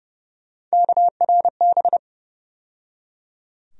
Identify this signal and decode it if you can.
Morse code, heard as a keyed tone, about 30 wpm, 700 Hz. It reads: KRB
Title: USA Call Sign Postfixes - 3 Letters 30wpm